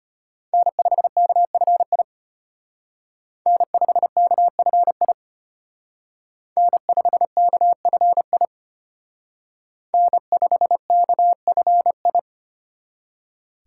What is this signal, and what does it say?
Morse code, heard as a keyed tone, about 32 wpm, 700 Hz. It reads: N5KFI N5KFI N5KFI N5KFI